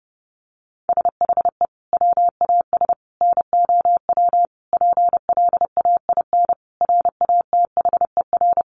Morse code, heard as a keyed tone, about 30 wpm, 700 Hz. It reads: SHE WAS NOW PLAIN RATHER
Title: Sentences from Top 1000 Words 30wpm